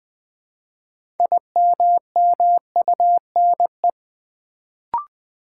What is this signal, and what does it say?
Morse code, heard as a keyed tone, about 20 wpm, 700 Hz. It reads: IMMUNE
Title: Build A Word - 2 Letters At A Time - 6-Letter Words - 20wpm